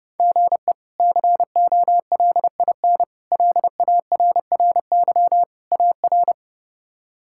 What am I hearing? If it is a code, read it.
Morse code, heard as a keyed tone, about 30 wpm, 700 Hz. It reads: GE COLIN LARRY AR
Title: K1USN SST QSO Practice - 30wpm